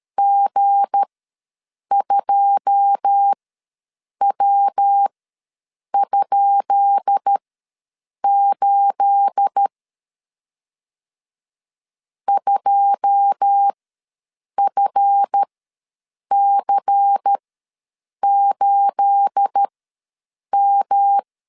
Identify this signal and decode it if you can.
Morse code, heard as a keyed tone, about 13 wpm, spaced as for 7 wpm, 800 Hz. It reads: G2W?8 2FC8M
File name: LEZIONI / 10 SIGNS / 10B INT ws 12wpm 3527